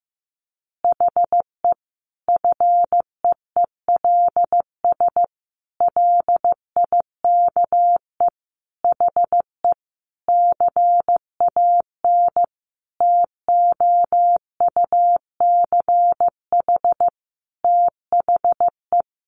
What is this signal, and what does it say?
Morse code, heard as a keyed tone, about 14 wpm, 700 Hz. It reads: HE FEELS LIKE HE CAN TOUCH THE
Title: Additional Easy Phrases 15wpm